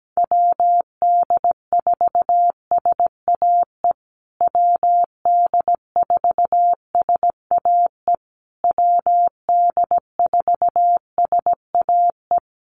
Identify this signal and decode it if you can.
Morse code, heard as a keyed tone, about 17 wpm, 700 Hz. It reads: WD4SAE WD4SAE WD4SAE
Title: USA Call Signs Repeated 3x 17wpm